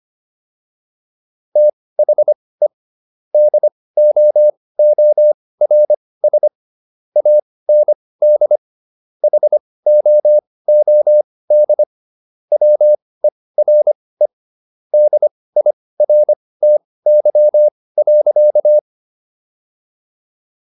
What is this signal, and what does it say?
Morse code, heard as a keyed tone, about 25 wpm, spaced as for 18 wpm, 600 Hz. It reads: THE DOORS AND HOOD WERE DIRTY.